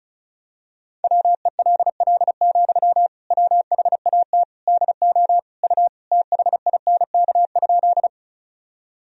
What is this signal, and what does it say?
Morse code, heard as a keyed tone, about 36 wpm, 700 Hz. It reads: WELL, WHAT DO U THINK?
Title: Sentences from Top 200 Words and Abbreviations 35wpm